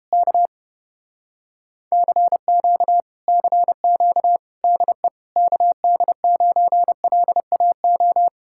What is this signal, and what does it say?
Morse code, heard as a keyed tone, about 32 wpm, 700 Hz. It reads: K CQ CQ DE KD9LAO